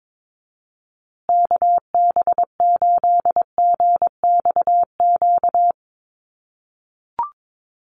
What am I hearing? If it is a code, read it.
Morse code, heard as a keyed tone, about 22 wpm, 700 Hz. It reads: KB8GXQ
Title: USA Call Signs Repeated 2x 22wpm